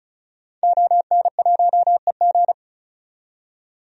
Morse code, heard as a keyed tone, about 36 wpm, 700 Hz. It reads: ON1EG
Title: DXCC-100 Call Signs - 5 Characters - 35wpm